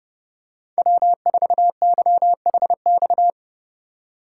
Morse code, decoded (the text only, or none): W4YHX